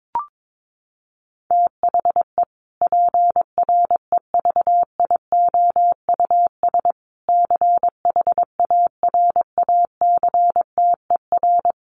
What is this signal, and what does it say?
Morse code, heard as a keyed tone, about 22 wpm, 700 Hz. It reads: THE PREVIOUS CHARACTER
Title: Sentences from Top 1500 Words 22wpm